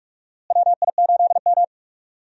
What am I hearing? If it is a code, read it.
Morse code, heard as a keyed tone, about 46 wpm, 700 Hz. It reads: WI8M
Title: USA Call Signs - 4 Characters - Repeated 2x - 45wpm